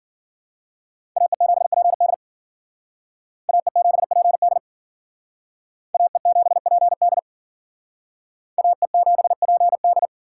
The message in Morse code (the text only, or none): AE7PD AE7PD AE7PD AE7PD